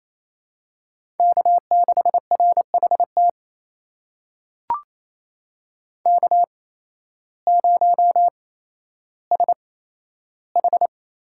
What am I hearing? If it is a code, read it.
Morse code, heard as a keyed tone, about 28 wpm, spaced as for 13 wpm, 700 Hz. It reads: K6RHT K0SH
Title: USA Call Signs - 8x Character Spacing - 28wpm